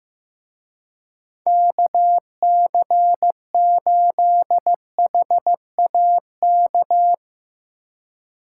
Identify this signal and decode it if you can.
Morse code, heard as a keyed tone, about 15 wpm, 700 Hz. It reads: KC8HAK
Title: Speed Racing - USA Call Signs 15wpm